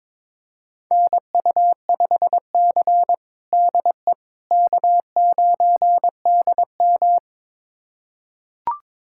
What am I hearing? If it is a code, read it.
Morse code, heard as a keyed tone, about 21 wpm, 700 Hz. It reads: NU5C DE K9DM
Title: USA Callsign DE Callsign - 4 Characters - 22wpm